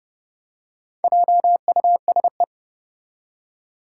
Morse code, heard as a keyed tone, about 30 wpm, 700 Hz. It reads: JUSE
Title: Four Challenging Letter-Number Combinations 30wpm